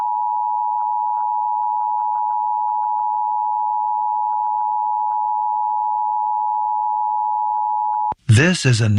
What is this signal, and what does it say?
Emergency Alert System (SAME) tones (the 853 and 960 Hz attention tone), with speech. It is heard from 0.2 to 8.0 s.